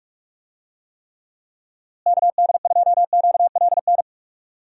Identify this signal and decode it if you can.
Morse code, heard as a keyed tone, about 46 wpm, 700 Hz. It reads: KD2QLN
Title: USA Call Signs - 6 Characters - Repeated 2x - 45wpm